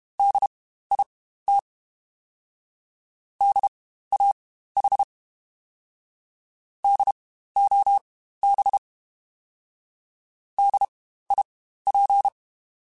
Morse code, heard as a keyed tone, about 32 wpm, spaced as for 15 wpm, 800 Hz. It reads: DIT DAH DOB DIP
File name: Spaced Alphas / Spaced Alphas mp3s / Ds sp @30